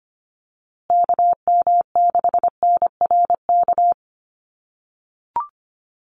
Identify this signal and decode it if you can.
Morse code, heard as a keyed tone, about 25 wpm, 700 Hz. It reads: KM6NRK
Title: USA Call Signs Repeated 2x 25wpm